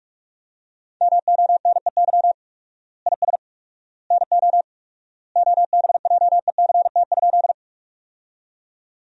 Morse code, heard as a keyed tone, about 46 wpm, 700 Hz. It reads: MONEY IS NO OBJECT?
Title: 4x Word-Spacing - Sentences from Top 500 Words - 45wpm